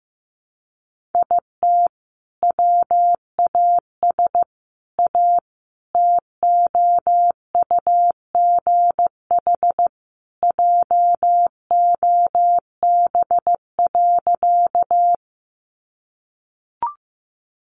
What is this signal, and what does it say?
Morse code, heard as a keyed tone, about 15 wpm, 700 Hz. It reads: IT WAS A TOUGH JOB.
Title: Sentences from Top 1200 Words 15wpm